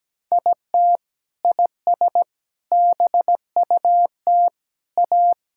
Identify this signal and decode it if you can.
Morse code, heard as a keyed tone, about 17 wpm, 700 Hz. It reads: IT IS BUT A
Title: Sentences from Top 1000 Words - Review Part 5 - 17wpm